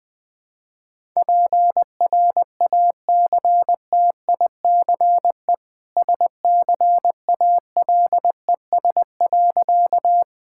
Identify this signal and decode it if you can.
Morse code, heard as a keyed tone, about 20 wpm, 700 Hz. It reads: PRACTICE SCALES.
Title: Easy Sentences with Hints - 20wpm